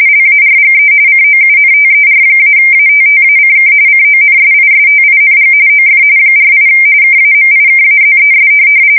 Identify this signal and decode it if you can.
RTTY, about 45.45 baud. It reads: LARGE COUNTER-ATTACK CONSISTING OF 48 KAMIKAZES